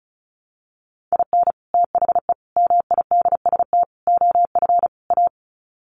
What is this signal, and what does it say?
Morse code, heard as a keyed tone, about 34 wpm, 700 Hz. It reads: IN THE MIDST OF A
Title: Sets of 5 Words 35wpm